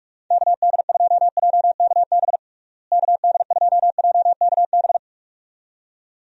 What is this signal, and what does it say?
Morse code, heard as a keyed tone, about 43 wpm, 700 Hz. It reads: KD2JKB KD2JKB
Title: USA Call Signs - 6 Characters - Repeated 2x - 45wpm